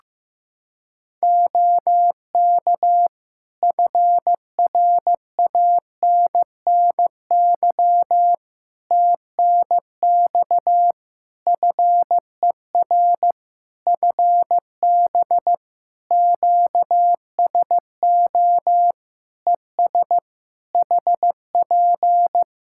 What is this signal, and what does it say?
Morse code, heard as a keyed tone, about 15 wpm, 700 Hz. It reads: OK FRANNY TNX FER FB QSO ES HP
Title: Practice QSO - Exchange 4 - Ending - 15wpm